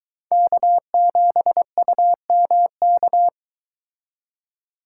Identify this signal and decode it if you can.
Morse code, heard as a keyed tone, about 23 wpm, 700 Hz. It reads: K7UMK